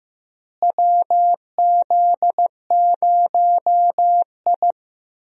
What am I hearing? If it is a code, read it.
Morse code, heard as a keyed tone, about 15 wpm, 700 Hz. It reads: WZ0I
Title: USA Call Signs - 4 Characters - Repeated 3x - 15wpm